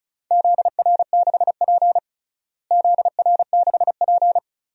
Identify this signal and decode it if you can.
Morse code, heard as a keyed tone, about 36 wpm, 700 Hz. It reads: ZR6P ZR6P
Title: DXCC-100 Call Signs - 4 Characters - Repeated 3x - 35wpm